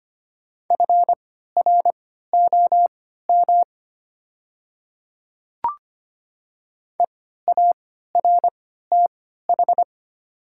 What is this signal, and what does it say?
Morse code, heard as a keyed tone, about 25 wpm, spaced as for 14 wpm, 700 Hz. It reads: FROM EARTH
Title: 3x Character Spacing - Top 300 Words - 25wpm